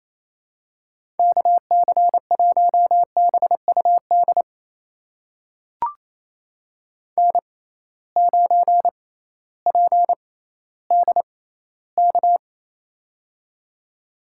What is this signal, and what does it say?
Morse code, heard as a keyed tone, about 28 wpm, spaced as for 15 wpm, 700 Hz. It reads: KC1BUD N9PDK
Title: USA Call Signs - 6x Character Spacing - 28wpm